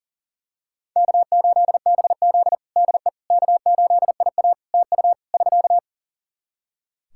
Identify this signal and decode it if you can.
Morse code, heard as a keyed tone, about 40 wpm, 700 Hz. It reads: K8BZ DE K8IA TU <SK>
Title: K1USN SST QSO Practice 40wpm